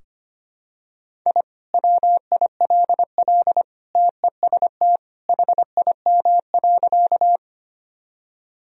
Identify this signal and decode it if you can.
Morse code, heard as a keyed tone, about 25 wpm, 700 Hz. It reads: I WILL TEST HIM.
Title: Sentences from Top 1000 Words - Review Part 3 - 25wpm